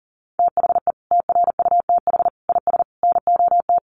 Morse code, heard as a keyed tone, about 40 wpm, 700 Hz. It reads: THE TRUTH IS NOT
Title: Sentences from Top 1500 Words 40wpm